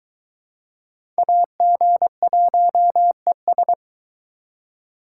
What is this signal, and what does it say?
Morse code, heard as a keyed tone, about 23 wpm, 700 Hz. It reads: AG1ES